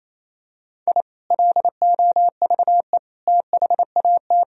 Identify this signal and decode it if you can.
Morse code, heard as a keyed tone, about 27 wpm, 700 Hz. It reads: I LOVE THAT
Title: Sentences from Top 1000 Words - Review Part 4 - 28wpm